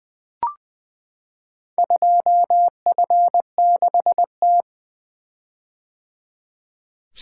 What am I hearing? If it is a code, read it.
Morse code, heard as a keyed tone, about 20 wpm, 700 Hz. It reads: 2F6T
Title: Four Letter-Number Combinations - 20wpm